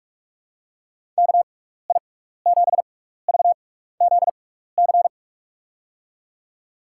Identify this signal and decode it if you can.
Morse code, heard as a keyed tone, about 43 wpm, spaced as for 15 wpm, 700 Hz. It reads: KI7VZC
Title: USA Call Signs - 6x Character Spacing - 45wpm